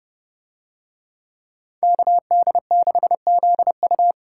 Morse code, heard as a keyed tone, about 30 wpm, 700 Hz. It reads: KD6ZU